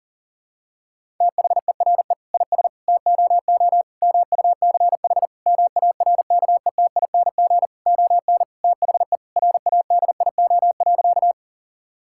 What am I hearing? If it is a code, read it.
Morse code, heard as a keyed tone, about 40 wpm, 700 Hz. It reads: THERE IS TOO MUCH MARKETING ON THE RADIO.
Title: Sentences from Top 1000 Words - Review Part 5 - 40wpm